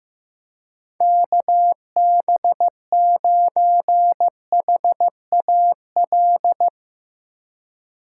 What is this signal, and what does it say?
Morse code, heard as a keyed tone, about 15 wpm, 700 Hz. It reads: KB9HAL
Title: Speed Racing - USA 2x3 Call Signs 15wpm